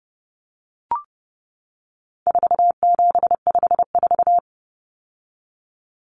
Morse code, heard as a keyed tone, about 30 wpm, 700 Hz. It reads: 4754
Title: Common Cut Numbers - 4-Digit 30wpm